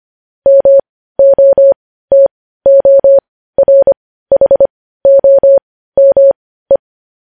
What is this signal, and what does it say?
Morse code, heard as a keyed tone, about 25 wpm, spaced as for 15 wpm, 550 Hz. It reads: MOTORHOME
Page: News headlines in Morse code at 15 wpm